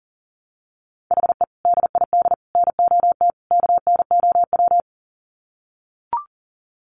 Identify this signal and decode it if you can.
Morse code, heard as a keyed tone, about 40 wpm, 700 Hz. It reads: HE DID NOT KNOW
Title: Sets of 4 Words 40wpm